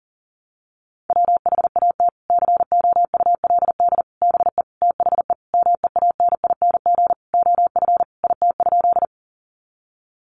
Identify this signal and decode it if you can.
Morse code, heard as a keyed tone, about 40 wpm, 700 Hz. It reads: WHAT COULD BE THE MEANING OF IT?
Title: Sentences from Top 900 Words 40wpm